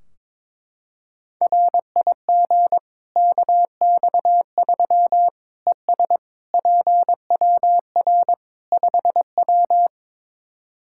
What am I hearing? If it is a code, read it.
Morse code, heard as a keyed tone, about 22 wpm, 700 Hz. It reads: RIG KX3 ES PWR 5W
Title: Practice QSO - Exchange 2 - Info - 22wpm